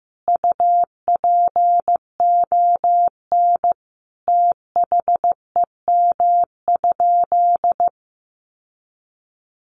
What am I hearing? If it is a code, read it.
Morse code, heard as a keyed tone, about 15 wpm, 700 Hz. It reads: UPON THEM?
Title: Sentences from Top 1300 Words 15wpm